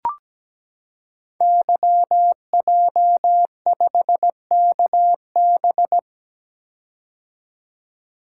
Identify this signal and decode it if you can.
Morse code, heard as a keyed tone, about 17 wpm, 700 Hz. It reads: YJ5KB
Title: Code Groups - 5 Letter:Numbers - 17wpm